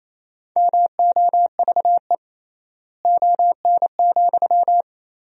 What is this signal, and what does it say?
Morse code, heard as a keyed tone, about 28 wpm, 700 Hz. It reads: MOVE ON,
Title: 3x Word-Spacing - Sentences from Top 200 Words - 28wpm